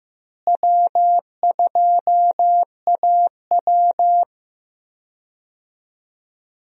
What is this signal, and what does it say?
Morse code, heard as a keyed tone, about 15 wpm, 700 Hz. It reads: W2AW
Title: USA Call Signs - 4 Characters - Repeated 3x - 15wpm